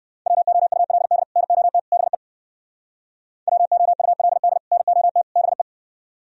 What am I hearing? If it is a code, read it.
Morse code, heard as a keyed tone, about 58 wpm, 700 Hz. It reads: WOULD NOT BE WOULD NOT BE